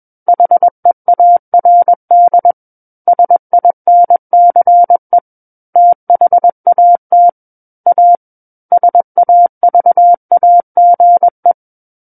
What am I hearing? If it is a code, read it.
Morse code, heard as a keyed tone, about 21 wpm, 700 Hz. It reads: HEARD SINCE THAT A SAVAGE
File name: Book One / 12-Chapter 12 - 21 WPM